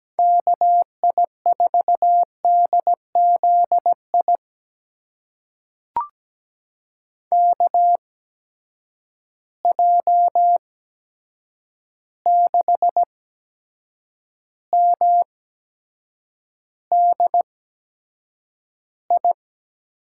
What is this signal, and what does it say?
Morse code, heard as a keyed tone, about 17 wpm, spaced as for 7 wpm, 700 Hz. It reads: KI4DZI KJ6MDI
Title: USA Call Signs - 6 Characters - 8x Character Spacing - 17wpm